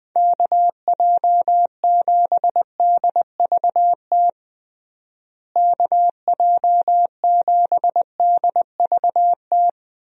Morse code, heard as a keyed tone, about 21 wpm, 700 Hz. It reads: KJ7DVT KJ7DVT
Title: USA Call Signs - 6 Characters - Repeated 3x - 20wpm